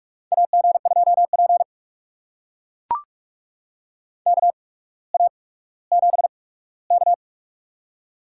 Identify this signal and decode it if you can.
Morse code, heard as a keyed tone, about 44 wpm, spaced as for 19 wpm, 700 Hz. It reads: AG2P KA7K
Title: USA Call Signs - 4 Characters - 8x Character Spacing - 45wpm